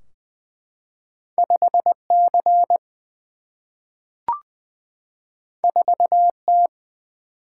Morse code, heard as a keyed tone, about 20 wpm, 700 Hz. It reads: 5C 4T
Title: Single-Letter Single-Number 20wpm